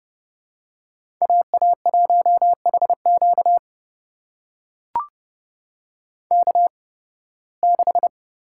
Morse code, heard as a keyed tone, about 31 wpm, spaced as for 20 wpm, 700 Hz. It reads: AA1HQ K6
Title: USA Call Signs - 5 Characters - 8x Character Spacing - 30wpm